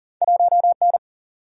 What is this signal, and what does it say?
Morse code, heard as a keyed tone, about 40 wpm, 700 Hz. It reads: 1 N
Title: Single-Letter Single-Number 40wpm